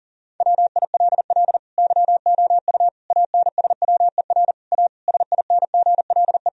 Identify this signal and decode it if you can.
Morse code, heard as a keyed tone, about 40 wpm, 700 Hz. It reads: WILL YOU ANSWER A SINGLE
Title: Sentences from Top 1000 Words - Review Part 4 - 40wpm